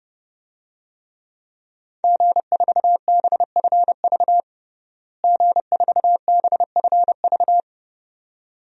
Morse code, heard as a keyed tone, about 30 wpm, 700 Hz. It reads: G4BFV G4BFV
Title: DXCC-100 Call Signs - 5 Characters - Repeated 2x - 30wpm